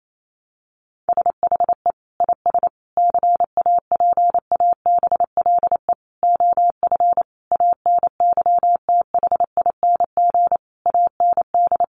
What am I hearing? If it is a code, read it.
Morse code, heard as a keyed tone, about 28 wpm, 700 Hz. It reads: SHE IS CAPABLE OF ANYTHING AND
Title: Sentences from Top 1200 Words 28wpm